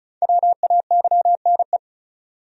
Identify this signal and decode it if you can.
Morse code, heard as a keyed tone, about 34 wpm, 700 Hz. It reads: WAYNE
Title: USA Common Names - 35wpm